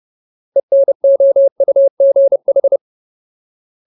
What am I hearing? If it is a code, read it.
Morse code, heard as a keyed tone, about 30 wpm, 550 Hz. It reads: ENOUGH
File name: Ultimate Word List / 30WPM Ultimate